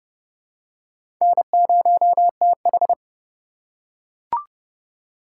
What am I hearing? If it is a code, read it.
Morse code, heard as a keyed tone, about 26 wpm, 700 Hz. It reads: N0TH